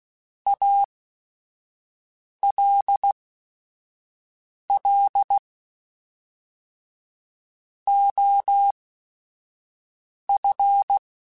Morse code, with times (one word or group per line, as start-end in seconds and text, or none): 0.5-5.4 ALL
7.9-11.0 OF